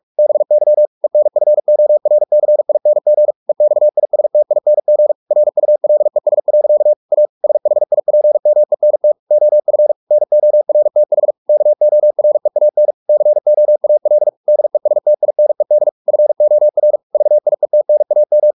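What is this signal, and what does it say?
Morse code, heard as a keyed tone, about 44 wpm, 600 Hz. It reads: BY ENFORCING EXISTING RULES. A SHIPMENT OF NORTH KOREAN COAL DESTINED FOR VIETNAM
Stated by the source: Morse practice at 45 wpm, the novel Never by Ken Follett in English, from CW med Gnister.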